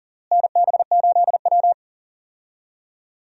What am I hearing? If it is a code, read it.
Morse code, heard as a keyed tone, about 37 wpm, 700 Hz. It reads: NB8W
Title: USA Call Signs - 4 Characters - Repeated 2x - 40wpm